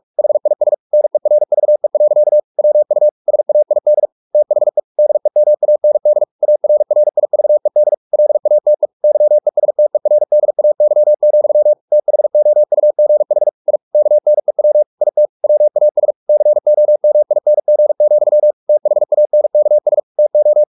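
Morse code, heard as a keyed tone, about 44 wpm, 600 Hz. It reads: HIS NERVE. WU SAID THE DEMAND ARRIVED LATE YESTERDAY, THOUGH I KNEW IT WAS COMING, THANKS TO
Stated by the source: Morse practice at 45 wpm, the novel Never by Ken Follett in English, from CW med Gnister.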